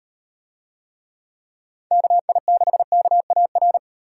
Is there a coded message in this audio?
Yes: Morse code from 1.9 to 3.8 s, heard as a keyed tone, about 38 wpm, 700 Hz. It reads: KI6KAR